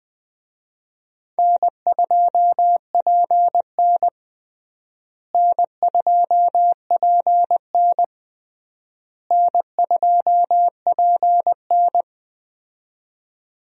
Morse code, heard as a keyed tone, about 21 wpm, 700 Hz. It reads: N2PN N2PN N2PN
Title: USA Call Signs - 4 Characters - Repeated 3x - 20wpm